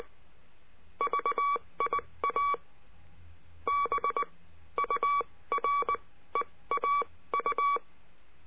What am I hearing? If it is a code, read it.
Morse code, heard as a keyed tone, about 19 wpm, spaced as for 15 wpm, 1140 Hz. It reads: VIA BUREAU
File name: CURSO DE TELEGRAFIA